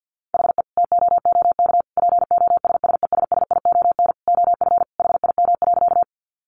Morse code, heard as a keyed tone, about 50 wpm, 700 Hz. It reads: HE TOOK POSSESSION OF HIM.
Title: Sentences from Top 1400 Words 50wpm